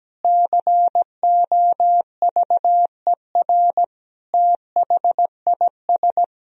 Morse code, heard as a keyed tone, about 17 wpm, 700 Hz. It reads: COVER THIS
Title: Sentences from Top 1000 Words - Review Part 1 - 17wpm